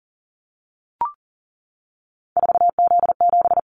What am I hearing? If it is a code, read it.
Morse code, heard as a keyed tone, about 40 wpm, 700 Hz. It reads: 4Z7
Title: Callsign Prefixes for Top 50 DXCC - 40wpm